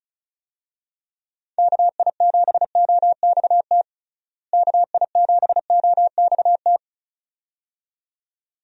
Morse code, heard as a keyed tone, about 36 wpm, 700 Hz. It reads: KI7OXT KI7OXT
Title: USA Call Signs - 6 Characters - Repeated 2x - 35wpm